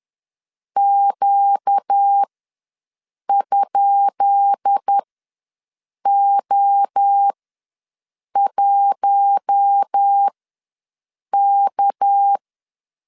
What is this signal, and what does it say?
Morse code, heard as a keyed tone, about 10 wpm, spaced as for 6 wpm, 800 Hz. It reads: Q?O1K
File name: LEZIONI / 10 SIGNS / 10B INT ws 10wpm 4134